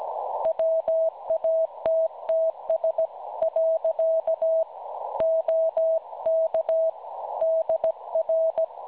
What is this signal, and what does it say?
Morse code, heard as a keyed tone, about 16 wpm, 650 Hz. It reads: WATTS . OK DR
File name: cwronde / websdr recording .6kHz